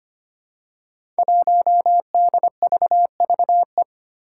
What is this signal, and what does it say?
Morse code, heard as a keyed tone, about 25 wpm, 700 Hz. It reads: 1DVVE
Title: Five Challenging Letter-Number Combinations 25wpm